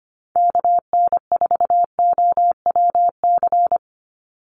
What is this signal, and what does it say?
Morse code, heard as a keyed tone, about 25 wpm, 700 Hz. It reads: KN4OWC
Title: USA Call Signs 25wpm